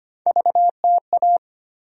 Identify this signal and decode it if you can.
Morse code, heard as a keyed tone, about 25 wpm, 700 Hz. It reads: VTA
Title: USA Call Sign Postfixes - 3 Letters 25wpm